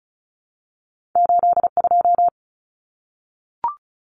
Morse code, heard as a keyed tone, about 36 wpm, 700 Hz. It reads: 82
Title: Numbers - 2-Digit 35wpm